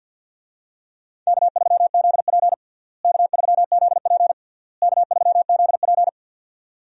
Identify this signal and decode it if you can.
Morse code, heard as a keyed tone, about 48 wpm, 700 Hz. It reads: K3ZP K3ZP K3ZP
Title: USA Call Signs - 4 Characters - Repeated 3x - 50wpm